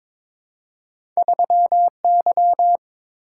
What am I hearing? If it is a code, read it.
Morse code, heard as a keyed tone, about 22 wpm, 700 Hz. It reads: 3Y
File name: practice / 22wpm / Two Letter-Number Combinations - 22wpm